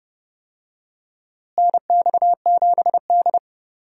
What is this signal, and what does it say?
Morse code, heard as a keyed tone, about 30 wpm, 700 Hz. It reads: NX7D
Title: USA Call Signs - 4 Characters - Repeated 2x - 30wpm